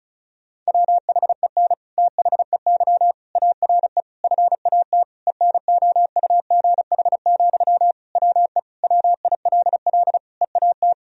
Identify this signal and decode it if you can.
Morse code, heard as a keyed tone, about 36 wpm, 700 Hz. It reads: WHEN THEY ARE FAT ENOUGH, WE WILL EAT
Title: Sentences from Top 1000 Words - Review Part 4 - 35wpm